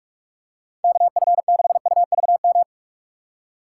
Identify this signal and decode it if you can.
Morse code, heard as a keyed tone, about 47 wpm, 700 Hz. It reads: KF6UVM
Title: USA Call Signs - 2x Character Spacing - 45wpm